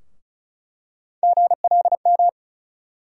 Morse code, heard as a keyed tone, about 34 wpm, 700 Hz. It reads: GLM
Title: USA Call Sign Postfixes - 3 Letters 35wpm